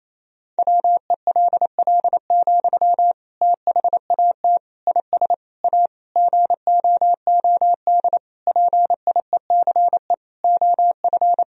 Morse code, heard as a keyed tone, about 27 wpm, 700 Hz. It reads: WELL, THAT IS A GOOD PIECE OF
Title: Sentences from Top 1000 Words - Review Part 2 - 28wpm